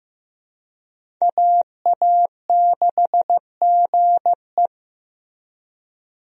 Morse code, heard as a keyed tone, about 16 wpm, 700 Hz. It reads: AA6GE
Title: USA Call Signs - 5 Characters - Repeated 3x - 15wpm